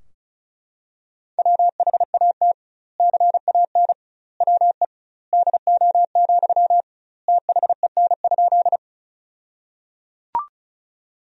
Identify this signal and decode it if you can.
Morse code, heard as a keyed tone, about 36 wpm, 700 Hz. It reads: WHAT CAN WE DO, THEN?
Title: 2x Word-Spacing - Sentences from Top 100 Words - 35wpm